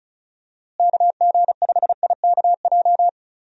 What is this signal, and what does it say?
Morse code, heard as a keyed tone, about 36 wpm, 700 Hz. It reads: KG5IKJ
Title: USA Call Signs - 6 Characters - Repeated 2x - 35wpm